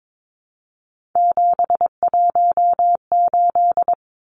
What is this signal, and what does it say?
Morse code, heard as a keyed tone, about 22 wpm, 700 Hz. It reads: 718
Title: Numbers - 3-Digit 22wpm